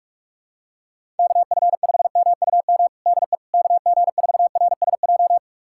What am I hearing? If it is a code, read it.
Morse code, heard as a keyed tone, about 46 wpm, 700 Hz. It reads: KF5MUM DE KG4RSJ
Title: USA Callsign DE Callsign - 6 Characters - 45wpm